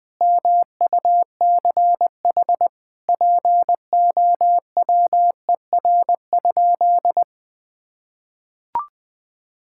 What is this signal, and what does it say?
Morse code, heard as a keyed tone, about 21 wpm, 700 Hz. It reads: MUCH POWER?
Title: Sentences from Top 1000 Words - Review Part 5 - 20wpm